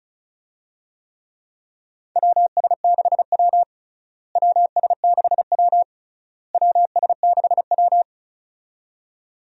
Morse code, heard as a keyed tone, about 34 wpm, 700 Hz. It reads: WS6W WS6W WS6W
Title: USA Call Signs - 4 Characters - Repeated 3x - 35wpm